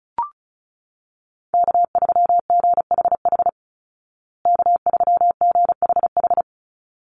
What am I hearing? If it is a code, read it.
Morse code, heard as a keyed tone, about 34 wpm, 700 Hz. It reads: K3GHH K3GHH
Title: Call Signs Repeated 3x - 35wpm